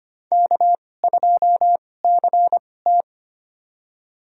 Morse code, heard as a keyed tone, about 25 wpm, spaced as for 18 wpm, 700 Hz. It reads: K2CT
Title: USA Call Signs - 4 Characters - 2x Character Spacing - 25wpm